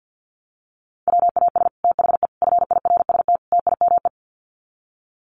Morse code, heard as a keyed tone, about 48 wpm, 700 Hz. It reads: WAS THE FIRST TIME
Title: Sets of 4 Words 50wpm